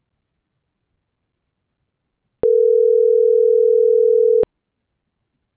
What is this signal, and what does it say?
Ringback tone (North America), 440 + 480 Hz.
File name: CENTRAL MABE / out manuel.posadas - .48387